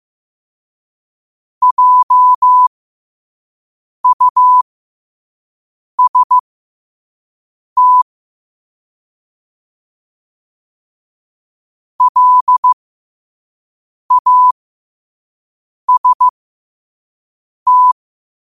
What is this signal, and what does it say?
Morse code, heard as a keyed tone, about 15 wpm, spaced as for 5 wpm, 1000 Hz. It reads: JUST LAST
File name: qotdfiles / 5wpm